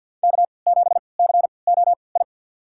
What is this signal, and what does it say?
Morse code, heard as a keyed tone, about 53 wpm, spaced as for 29 wpm, 690 Hz. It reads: K7COI
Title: USA Call Signs - 5 Characters - 3x Character Spacing - 50wpm